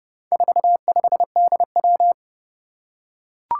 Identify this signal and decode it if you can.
Morse code, heard as a keyed tone, about 30 wpm, 700 Hz. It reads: 45DW
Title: Four Challenging Letter-Number Combinations 30wpm